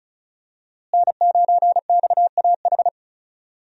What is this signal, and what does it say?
Morse code, heard as a keyed tone, about 36 wpm, 700 Hz. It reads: N9XAH
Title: USA Call Signs - 5 Characters - Repeated 2x - 35wpm